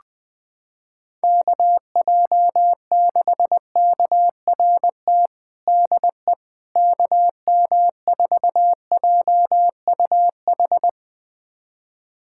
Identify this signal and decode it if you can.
Morse code, heard as a keyed tone, about 20 wpm, 700 Hz. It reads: KJ6KRT DE KM4JUH
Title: USA Callsign DE Callsign 20wpm